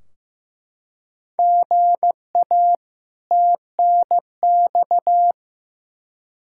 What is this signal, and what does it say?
Morse code, heard as a keyed tone, about 15 wpm, 700 Hz. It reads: GA TNX
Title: Practice QSO - Exchange 1 - Beginning - 15wpm